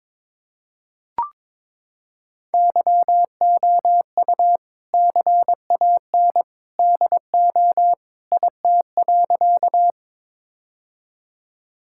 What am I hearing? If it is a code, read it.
Morse code, heard as a keyed tone, about 22 wpm, 700 Hz. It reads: YOU CAN DO IT.
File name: practice / 22wpm / Sentences with 2-Word Phrases - 22wpm